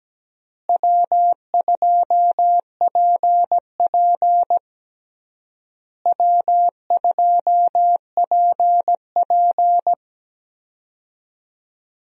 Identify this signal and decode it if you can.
Morse code, heard as a keyed tone, about 17 wpm, 700 Hz. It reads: W2PP W2PP
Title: USA Call Signs - 4 Characters - Repeated 3x - 17wpm